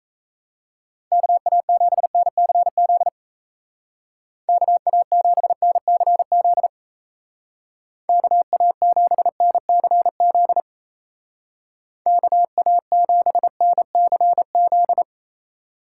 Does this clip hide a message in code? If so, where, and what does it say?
Yes: Morse code from 1.1 to 15.0 s, heard as a keyed tone, about 35 wpm, 700 Hz. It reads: KA7NCZ KA7NCZ KA7NCZ KA7NCZ